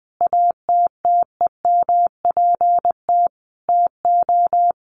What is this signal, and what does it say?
Morse code, heard as a keyed tone, about 20 wpm, 700 Hz. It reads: ATTEMPT TO
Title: Sentences from Top 1400 Words 20wpm